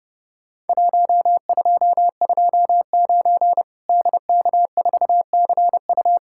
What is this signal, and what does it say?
Morse code, heard as a keyed tone, about 30 wpm, 700 Hz. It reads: 1229 DK4CU
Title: Worked All Europe DX Contest - QTC Exchange - 30wpm